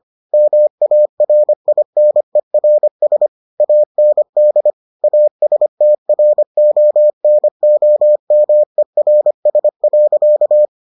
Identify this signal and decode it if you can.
Morse code, heard as a keyed tone, about 25 wpm, 600 Hz. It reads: MARINERS AND ASTRONOMERS.